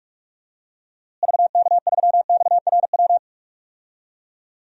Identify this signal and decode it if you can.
Morse code, heard as a keyed tone, about 42 wpm, 700 Hz. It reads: VK3XRW
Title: DXCC-100 Call Signs - 6 Characters - Repeated 2x - 45wpm